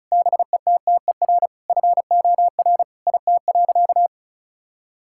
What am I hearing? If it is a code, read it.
Morse code, heard as a keyed tone, about 36 wpm, 700 Hz. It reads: BETTER FOR IT.
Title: Sentences from Top 1000 Words - Review Part 5 - 35wpm